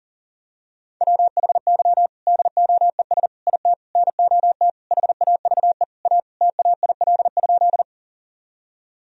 Morse code, heard as a keyed tone, about 40 wpm, 700 Hz. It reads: WHY DOES IT NOT HAVE A TAIL?
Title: Sentences from Top 1000 Words - Review Part 3 - 40wpm